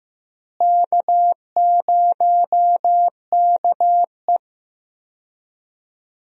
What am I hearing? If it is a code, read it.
Morse code, heard as a keyed tone, about 15 wpm, 700 Hz. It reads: K0KE
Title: USA Call Signs - 4 Characters - Repeated 2x - 15wpm